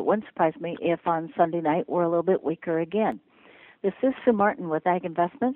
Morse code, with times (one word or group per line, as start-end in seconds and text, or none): none